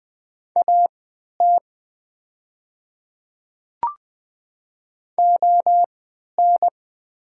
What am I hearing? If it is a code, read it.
Morse code, heard as a keyed tone, about 20 wpm, spaced as for 11 wpm, 700 Hz. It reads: AT ON
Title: 3x Character Spacing - Top 100 Words - 20wpm